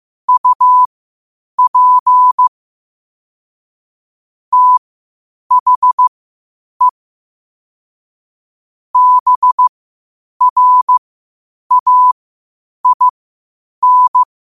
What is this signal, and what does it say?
Morse code, heard as a keyed tone, about 15 wpm, spaced as for 8 wpm, 1000 Hz. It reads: UP THE BRAIN
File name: qotdfiles / 8wpm